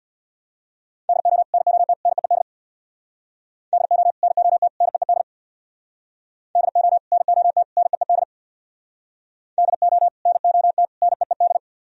Morse code, heard as a keyed tone, about 64 wpm, 700 Hz. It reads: DO NOT NEED DO NOT NEED DO NOT NEED DO NOT NEED